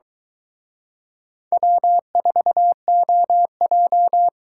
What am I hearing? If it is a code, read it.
Morse code, heard as a keyed tone, about 23 wpm, 700 Hz. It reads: W4OJ